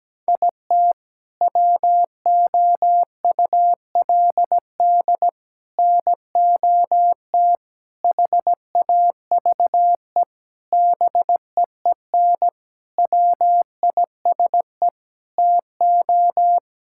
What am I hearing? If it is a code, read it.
Morse code, heard as a keyed tone, about 17 wpm, 700 Hz. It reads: IT WOULD NOT HAVE BEEN WISE TO
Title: Sentences from Top 1600 Words 17wpm